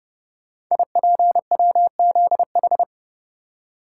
Morse code, heard as a keyed tone, about 30 wpm, 700 Hz. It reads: IPWZH
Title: Code Groups - 5 Letters - 30wpm